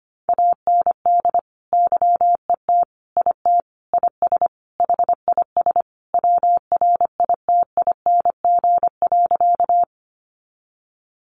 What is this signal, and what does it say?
Morse code, heard as a keyed tone, about 25 wpm, 700 Hz. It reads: AND YET IT IS HIS WRITING.
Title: Sentences from Top 800 Words 25wpm